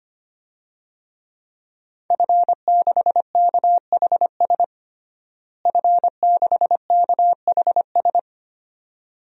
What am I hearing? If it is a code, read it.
Morse code, heard as a keyed tone, about 25 wpm, 700 Hz. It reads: F6KHS F6KHS
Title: DXCC-100 Call Signs - 5 Characters - Repeated 2x - 25wpm